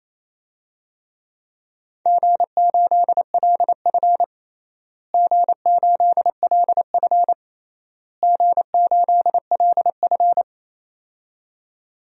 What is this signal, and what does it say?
Morse code, heard as a keyed tone, about 28 wpm, 700 Hz. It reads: G8LF G8LF G8LF
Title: DXCC-100 Call Signs - 4 Characters - Repeated 3x - 28wpm